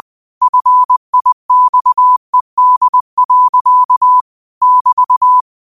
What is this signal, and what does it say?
Morse code, heard as a keyed tone, about 20 wpm, 1000 Hz. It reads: FIXED. <BT>
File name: qotdfiles / 20wpm